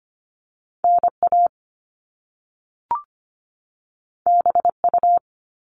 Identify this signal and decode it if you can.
Morse code, heard as a keyed tone, about 25 wpm, 700 Hz. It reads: NA BU
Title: Top Two Letter Combinations in Words 25wpm